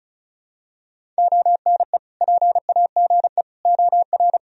Morse code, heard as a keyed tone, about 34 wpm, 690 Hz. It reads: ONE PAGE OR
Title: Sentences from Top 1000 Words - Review Part 1 - 35wpm